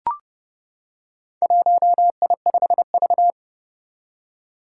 Morse code, heard as a keyed tone, about 30 wpm, 700 Hz. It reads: 1I5V
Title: Four Challenging Letter-Number Combinations 30wpm